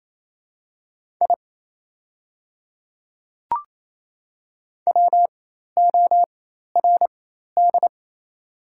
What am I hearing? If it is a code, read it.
Morse code, heard as a keyed tone, about 28 wpm, spaced as for 13 wpm, 700 Hz. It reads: I WORD
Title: 4x Character Spacing - Top 100 Words - 28wpm